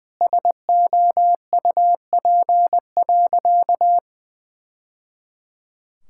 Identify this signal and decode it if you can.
Morse code, heard as a keyed tone, about 20 wpm, 700 Hz. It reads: SOUP.
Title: Sentences from Top 1000 Words - Review Part 6 - 20wpm